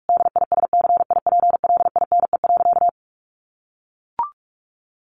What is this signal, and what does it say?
Morse code, heard as a keyed tone, about 44 wpm, 700 Hz. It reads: DISCIPLINE.
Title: Sentences from Top 900 Words 45wpm